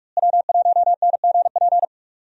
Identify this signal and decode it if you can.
Morse code, heard as a keyed tone, about 44 wpm, 700 Hz. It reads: W1NGP
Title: USA Call Signs - 5 Characters - Repeated 3x - 45wpm